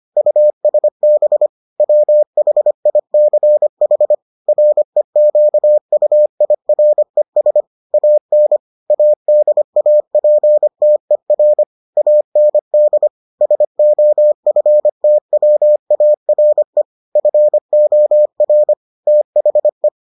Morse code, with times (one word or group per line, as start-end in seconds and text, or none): 0.2-1.5 USB
1.8-4.1 WHICH
4.5-7.6 REQUIRES
7.9-8.6 AN
8.9-11.6 ADAPTER
12.0-13.1 AND
13.4-16.8 SOFTWARE
17.2-18.7 FOR
19.1-19.9 THE